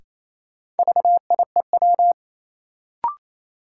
Morse code, heard as a keyed tone, about 28 wpm, 700 Hz. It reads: VIEW
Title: Sentences from Top 900 Words - New Word Drill - 28wpm